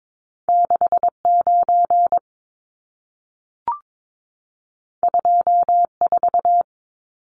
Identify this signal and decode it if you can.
Morse code, heard as a keyed tone, about 22 wpm, 700 Hz. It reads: 69 24
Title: Numbers - 2-Digit 22wpm